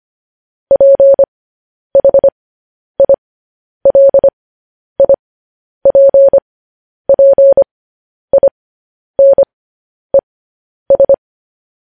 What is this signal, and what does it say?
Morse code, heard as a keyed tone, about 25 wpm, spaced as for 10 wpm, 550 Hz. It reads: PHILIPPINES